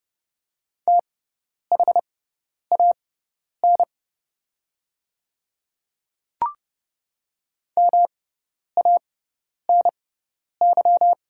Morse code, heard as a keyed tone, about 30 wpm, spaced as for 10 wpm, 700 Hz. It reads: THAN MANY
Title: 6x Character Spacing - Top 200 Words - 30wpm